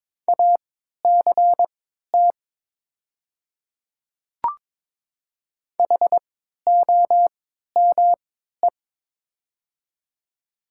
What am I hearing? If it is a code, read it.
Morse code, heard as a keyed tone, about 22 wpm, spaced as for 13 wpm, 700 Hz. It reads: ACT HOME
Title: 3x Character Spacing - Top 200 Words - 22wpm